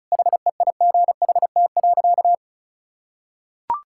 Morse code, heard as a keyed tone, about 36 wpm, 700 Hz. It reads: HEIGHT.
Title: Sentences from Top 1600 Words 35wpm